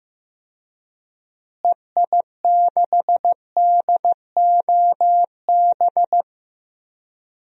Morse code, heard as a keyed tone, about 15 wpm, 700 Hz. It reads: EI6DOB
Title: DXCC-100 Call Signs - 6 Characters - Repeated 3x - 15wpm